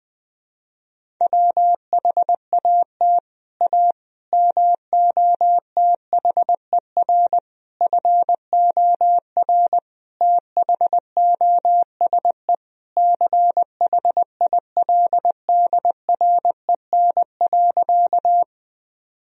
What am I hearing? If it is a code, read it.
Morse code, heard as a keyed tone, about 20 wpm, 700 Hz. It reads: WHAT A MOTHER FOR THOSE CHILDREN.
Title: Sentences from Top 1000 Words - Review Part 1 - 20wpm